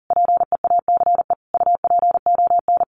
Morse code, heard as a keyed tone, about 40 wpm, 700 Hz. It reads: PEACE UPON
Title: Sentences from Top 1500 Words 40wpm